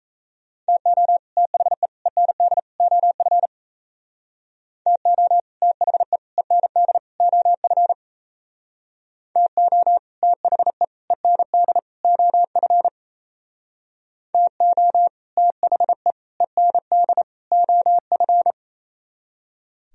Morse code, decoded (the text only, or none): TO THE END OF TO THE END OF TO THE END OF TO THE END OF